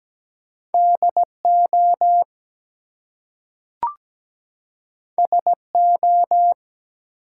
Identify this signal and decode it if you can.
Morse code, heard as a keyed tone, about 17 wpm, 700 Hz. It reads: DO SO
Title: Two Letter Words - 17wpm